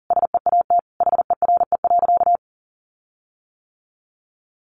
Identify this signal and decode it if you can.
Morse code, heard as a keyed tone, about 40 wpm, 700 Hz. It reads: SEAT HERE.
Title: Sentences from Top 1500 Words 40wpm